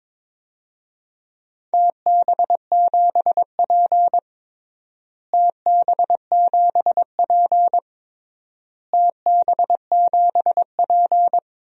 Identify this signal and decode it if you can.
Morse code, heard as a keyed tone, about 23 wpm, 700 Hz. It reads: TB7P TB7P TB7P
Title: DXCC-100 Call Signs - 4 Characters - Repeated 3x - 22wpm